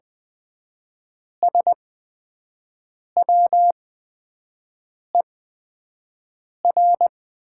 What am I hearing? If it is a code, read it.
Morse code, heard as a keyed tone, about 20 wpm, spaced as for 5 wpm, 700 Hz. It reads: SWER
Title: 8x Character Spacing - Top 500 Words - 20wpm